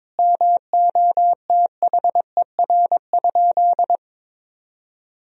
Morse code, heard as a keyed tone, about 22 wpm, 700 Hz. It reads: MOTHER?
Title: Sentences from Top 1000 Words - Review Part 2 - 22wpm